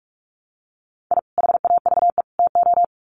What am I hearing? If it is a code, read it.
Morse code, heard as a keyed tone, about 44 wpm, 700 Hz. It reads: I HAVE TO
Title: Sets of 3 Words 45wpm